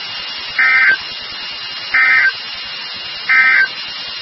Emergency Alert System (SAME) tones, heard as harsh buzzing bursts. It is heard from 0.5 to 3.7 s.
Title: NOAA Weather Radio WXK36